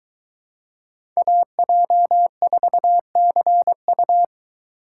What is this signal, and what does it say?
Morse code, heard as a keyed tone, about 22 wpm, 700 Hz. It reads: AJ4CU